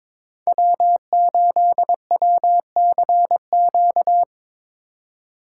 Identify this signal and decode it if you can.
Morse code, heard as a keyed tone, about 22 wpm, 700 Hz. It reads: W8WCQ
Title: USA Call Signs - 5 Characters - 3x Character Spacing - 22wpm